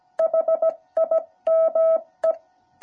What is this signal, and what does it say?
Morse code, heard as a keyed tone, about 17 wpm, 650 Hz. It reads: HIME